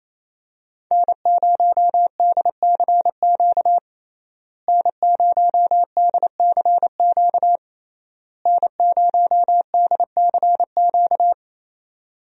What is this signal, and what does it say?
Morse code, heard as a keyed tone, about 28 wpm, 700 Hz. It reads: N0DCQ N0DCQ N0DCQ
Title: USA Call Signs - 5 Characters - Repeated 3x - 28wpm